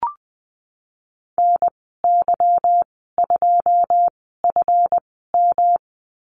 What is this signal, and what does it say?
Morse code, heard as a keyed tone, about 20 wpm, spaced as for 15 wpm, 700 Hz. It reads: NY2FM
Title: 154 Lesson 25g - Letter p - Call Signs 20wpm